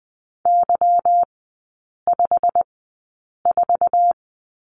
Morse code, heard as a keyed tone, about 20 wpm, spaced as for 8 wpm, 690 Hz. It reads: Y54
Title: 145 Lesson 24e - Letter y - Three Characters 20wpm